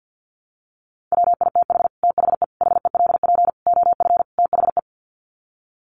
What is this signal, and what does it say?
Morse code, heard as a keyed tone, about 50 wpm, 700 Hz. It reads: WITH THE HELP OF THE
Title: Sets of 5 Words Encore 50wpm